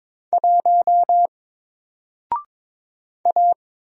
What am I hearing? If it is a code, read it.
Morse code, heard as a keyed tone, about 22 wpm, 700 Hz. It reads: 1 A
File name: Single-Letter-Number-22wpm